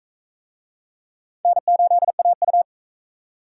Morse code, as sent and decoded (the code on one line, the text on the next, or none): -. ---.. .- ..-
N8AU